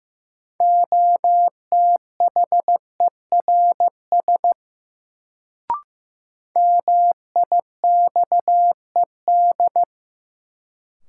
Morse code, heard as a keyed tone, about 15 wpm, 700 Hz. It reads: OTHERS MIXED
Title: Sentences from Top 1500 Words - New Word Drill Alt - 15wpm